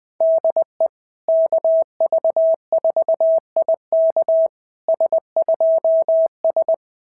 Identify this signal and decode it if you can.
Morse code, heard as a keyed tone, about 19 wpm, 650 Hz. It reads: DE KV4IK S2S
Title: SOTA QSO Practice 20wpm